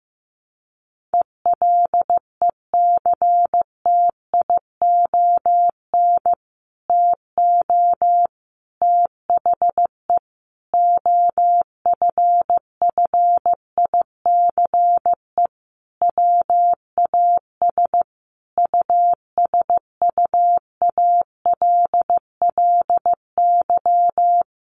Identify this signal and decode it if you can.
Morse code, heard as a keyed tone, about 15 wpm, 700 Hz. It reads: ELECTION TO THE OFFICE WAS USUALLY
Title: Sentences from Top 1500 Words 15wpm